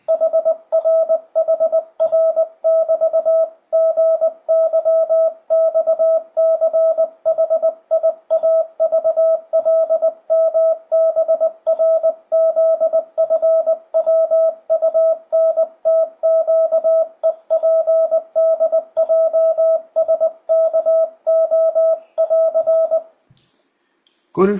Morse code, heard as a keyed tone, about 20 wpm, 660 Hz, with radio noise. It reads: HRHR<BT>GYXCHIAVLMBRZFWUNTQEPDJSKO<AR>
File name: 2025cw / speedup / 20wpm